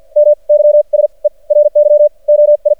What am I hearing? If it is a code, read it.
Morse code, heard as a keyed tone, about 50 wpm, spaced as for 35 wpm, 590 Hz. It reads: MYSERY OI
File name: archived / CRSnet